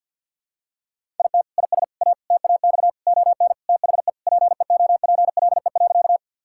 Morse code, heard as a keyed tone, about 51 wpm, 690 Hz. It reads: IT IS A TAX ON THE PEOPLE.
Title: Sentences from Top 1000 Words - Review Part 5 - 50wpm